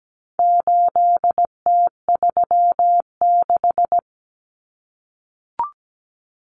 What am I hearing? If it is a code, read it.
Morse code, heard as a keyed tone, about 17 wpm, 700 Hz. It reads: 8T36
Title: Common Cut Numbers - 4-Digit 17wpm